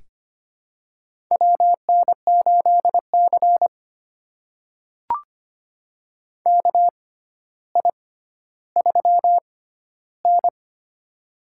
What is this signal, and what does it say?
Morse code, heard as a keyed tone, about 25 wpm, spaced as for 13 wpm, 700 Hz. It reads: WN8C KI3N
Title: USA Call Signs - 4 Characters - 6x Character Spacing - 25wpm